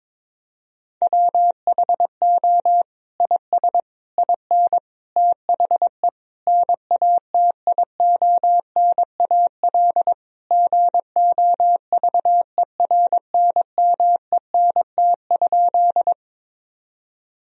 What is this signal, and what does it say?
Morse code, heard as a keyed tone, about 21 wpm, 700 Hz. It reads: WHO IS IN THE NATIONAL GOVERNMENT?
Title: Sentences from Top 1000 Words - Review Part 4 - 22wpm